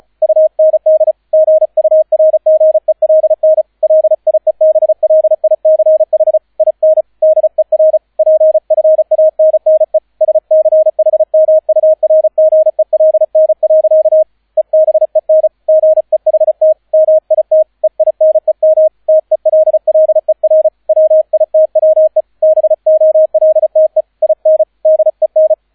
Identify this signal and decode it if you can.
Morse code, heard as a keyed tone, about 35 wpm, 630 Hz, with radio noise. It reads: UND GURGELN LIEBLICH IN DER PFANNE SCHMURGELN. EBEN GEHT MIT EINEM TELLER WITWE BOLTE IN DEN